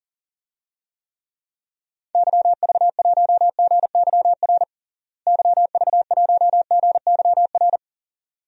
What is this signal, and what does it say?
Morse code, heard as a keyed tone, about 40 wpm, 700 Hz. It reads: YV1GYR YV1GYR
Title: DXCC-100 Call Signs - 6 Characters - Repeated 2x - 40wpm